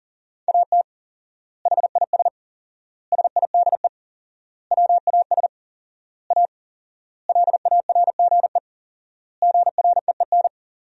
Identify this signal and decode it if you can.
Morse code, heard as a keyed tone, about 40 wpm, 700 Hz. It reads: AT HIS SIDE WAS A LARGE GREEN
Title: 4x Word-Spacing - Sentences from Top 500 Words - 40wpm